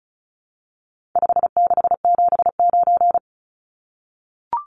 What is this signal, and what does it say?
Morse code, heard as a keyed tone, about 34 wpm, 700 Hz. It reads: 5679
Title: Numbers - 4-Digit 35wpm